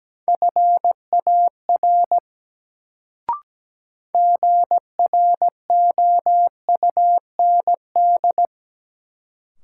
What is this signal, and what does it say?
Morse code, heard as a keyed tone, about 17 wpm, 700 Hz. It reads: FAR GROUND
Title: Sets of 3 Words - New Word Drill - 17wpm